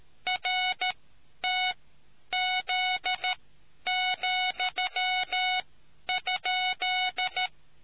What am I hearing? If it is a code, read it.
Morse code, heard as a keyed tone, about 13 wpm, spaced as for 10 wpm, 740 Hz. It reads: RTZ,?